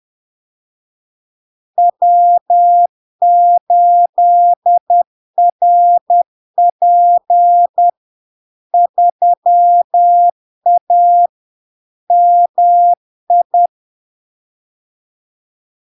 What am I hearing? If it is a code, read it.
Morse code, heard as a keyed tone, about 10 wpm, 700 Hz. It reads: W8RP 3A MI
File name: LARK-FD-2022-Log-in-CW-10wpm-700-Hz